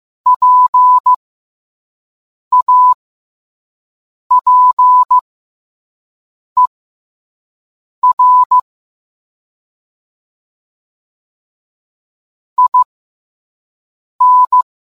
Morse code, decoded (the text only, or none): PAPER IN